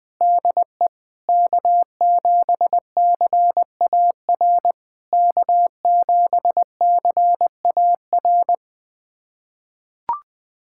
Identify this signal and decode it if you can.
Morse code, heard as a keyed tone, about 20 wpm, 690 Hz. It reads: DE K7CAR K7CAR
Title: POTA QSO Practice - 20wpm